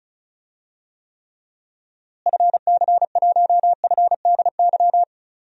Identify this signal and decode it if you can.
Morse code, heard as a keyed tone, about 34 wpm, 700 Hz. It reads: FC1FDY
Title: DXCC-100 Call Signs - 6 Characters - Repeated 3x - 35wpm